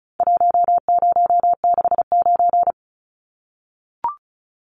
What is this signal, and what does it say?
Morse code, heard as a keyed tone, about 34 wpm, 700 Hz. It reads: 1069
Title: Numbers - Contest Style - 4-Digit 35wpm